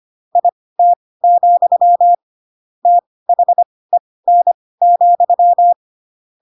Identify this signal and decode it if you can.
Morse code, heard as a keyed tone, about 25 wpm, spaced as for 18 wpm, 700 Hz. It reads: IT, THEN,
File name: Pinocchio / Pinocchio ch4 12 25 ik1hge